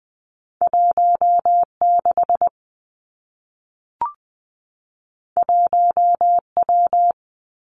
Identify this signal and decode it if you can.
Morse code, heard as a keyed tone, about 20 wpm, 700 Hz. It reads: 1 6 1 W
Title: Two Challenging Letter-Number Combinations 20wpm